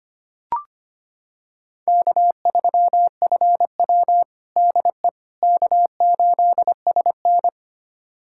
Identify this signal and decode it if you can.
Morse code, heard as a keyed tone, about 25 wpm, 700 Hz. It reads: K3FW DE K8SN
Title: USA Callsign DE Callsign - 4 Characters - 25wpm